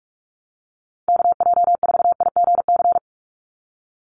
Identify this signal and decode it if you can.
Morse code, heard as a keyed tone, about 46 wpm, 700 Hz. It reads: KJ4IGC
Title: USA Call Signs 45wpm